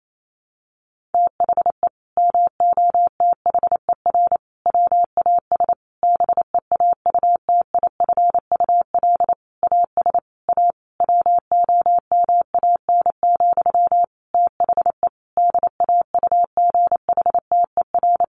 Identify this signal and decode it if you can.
Morse code, heard as a keyed tone, about 28 wpm, 700 Hz. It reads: THE MOTHER WAS BEAUTIFUL AS A WOMAN, THE DAUGHTER
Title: Sentences from Top 1500 Words 28wpm